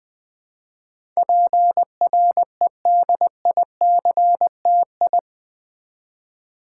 Morse code, heard as a keyed tone, about 20 wpm, 700 Hz. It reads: PREDICTI
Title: Build A Word - 4 Letters At A Time - 10-Letter Words - 20wpm